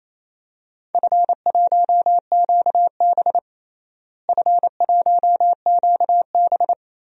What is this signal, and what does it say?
Morse code, heard as a keyed tone, about 28 wpm, 700 Hz. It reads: F1QB F1QB
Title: DXCC-100 Call Signs - 4 Characters - Repeated 2x - 28wpm